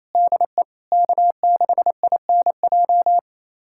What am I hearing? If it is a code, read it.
Morse code, heard as a keyed tone, about 28 wpm, 700 Hz. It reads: DE K6INJ
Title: USA Callsign DE Callsign - 5 Characters - 28wpm